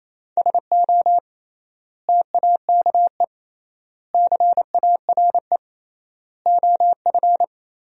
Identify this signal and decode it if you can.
Morse code, heard as a keyed tone, about 28 wpm, 700 Hz. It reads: SO TAKE CARE OF
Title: 3x Word-Spacing - Sentences from Top 300 Words - 28wpm